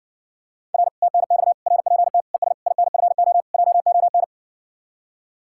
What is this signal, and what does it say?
Morse code, heard as a keyed tone, about 61 wpm, 700 Hz. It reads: R TNX RPT ES INFO JON